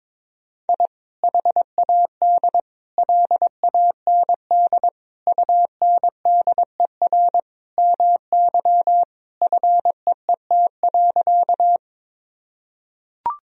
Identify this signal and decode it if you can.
Morse code, heard as a keyed tone, about 21 wpm, 700 Hz. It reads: I HAD LAND UNDER MY FEET.
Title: Sentences from Top 1000 Words - Review Part 1 - 22wpm